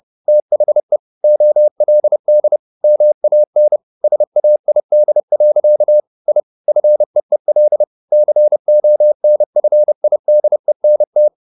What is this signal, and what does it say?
Morse code, heard as a keyed tone, about 30 wpm, 600 Hz. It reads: THE OLD MAN SAID. I FEEL CONFIDENT
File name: the old man and the sea / the old man and the sea 1 30wpm